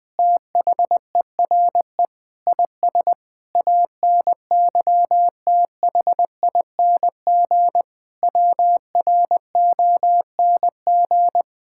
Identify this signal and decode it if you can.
Morse code, heard as a keyed tone, about 20 wpm, 700 Hz. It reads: THERE IS ANYTHING WRONG
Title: Sentences with 7-Word Phrases - 20wpm